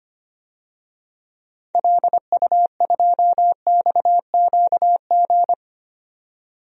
Morse code, heard as a keyed tone, about 25 wpm, 700 Hz. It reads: LU2XQG
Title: DXCC-100 Call Signs - 6 Characters - 25wpm